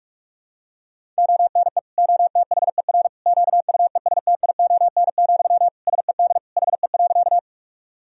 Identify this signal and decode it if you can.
Morse code, heard as a keyed tone, about 46 wpm, 700 Hz. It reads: ONE OTHER QUESTION, SED HE.
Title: Sentences from Top 400 Words and Abbreviations 45wpm